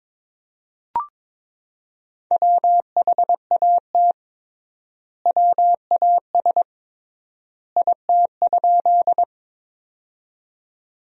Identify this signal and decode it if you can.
Morse code, heard as a keyed tone, about 22 wpm, 700 Hz. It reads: WHAT WAS IT?
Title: 3x Word-Spacing - Sentences from Top 100 Words - 22wpm